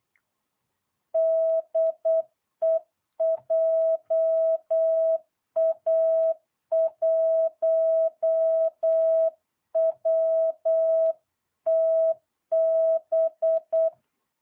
Morse code, heard as a keyed tone, about 8 wpm, 650 Hz. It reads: DE JA1WTB